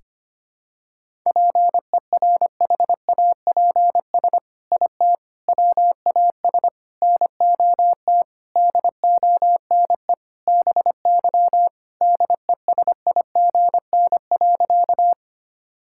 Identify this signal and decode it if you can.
Morse code, heard as a keyed tone, about 25 wpm, 700 Hz. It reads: PERHAPS IT WAS NOT DONE BY DESIGN.
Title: Sentences from Top 1000 Words - Review Part 5 - 25wpm